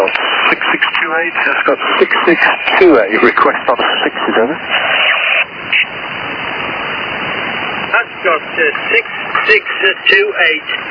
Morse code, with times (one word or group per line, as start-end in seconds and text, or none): none